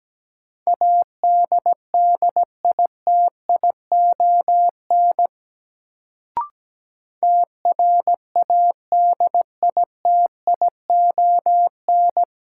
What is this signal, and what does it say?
Morse code, heard as a keyed tone, about 17 wpm, 700 Hz. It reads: ADDITION TRADITION
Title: Sentences from Top 1500 Words - New Word Drill Alt - 17wpm